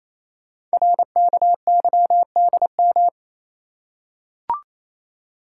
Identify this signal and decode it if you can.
Morse code, heard as a keyed tone, about 27 wpm, 700 Hz. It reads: RKYDM
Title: Code Groups - 5 Letters - 28wpm